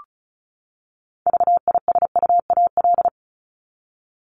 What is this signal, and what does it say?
Morse code, heard as a keyed tone, about 34 wpm, 700 Hz. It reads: VISUAL
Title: Top 751-1000 Words 35wpm